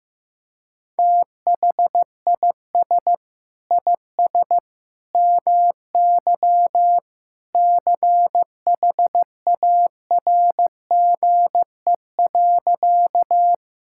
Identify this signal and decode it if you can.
Morse code, heard as a keyed tone, about 16 wpm, 700 Hz. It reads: THIS IS MY CHARGE.
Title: Sentences from Top 1000 Words - Review Part 6 - 15wpm